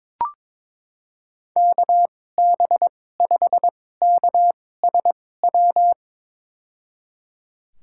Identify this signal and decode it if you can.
Morse code, heard as a keyed tone, about 22 wpm, spaced as for 16 wpm, 700 Hz. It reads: KB5KSW
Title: USA Call Signs - 6 Characters - 2x Character Spacing - 22wpm